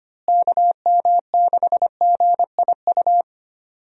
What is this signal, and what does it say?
Morse code, heard as a keyed tone, about 25 wpm, 700 Hz. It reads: KM6GIU
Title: USA Call Signs - 6 Characters - Repeated 2x - 25wpm